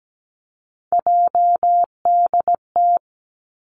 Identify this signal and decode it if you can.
Morse code, heard as a keyed tone, about 17 wpm, 700 Hz. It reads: JDT
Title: Three Challenging Letter-Number Combinations 17wpm